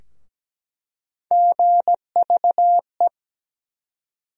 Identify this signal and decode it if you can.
Morse code, heard as a keyed tone, about 17 wpm, 700 Hz. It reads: GVE
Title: USA Call Sign Postfixes - 3 Letters 17wpm